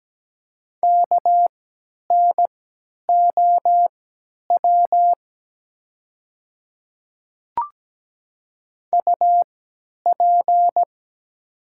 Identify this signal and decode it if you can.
Morse code, heard as a keyed tone, about 17 wpm, spaced as for 10 wpm, 700 Hz. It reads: KNOW UP
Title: 3x Character Spacing - Top 200 Words - 17wpm